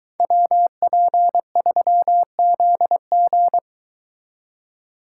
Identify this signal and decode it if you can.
Morse code, heard as a keyed tone, about 23 wpm, 700 Hz. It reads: WP3ZG